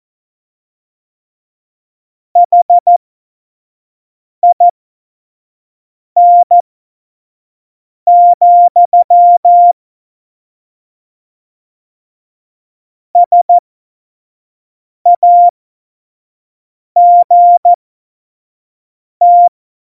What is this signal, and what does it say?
Morse code, heard as a keyed tone, about 14 wpm, spaced as for 5 wpm, 700 Hz. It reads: HIN, SAGT
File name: Morsekurs / Witze / Witz 08. September 2020, 25 F70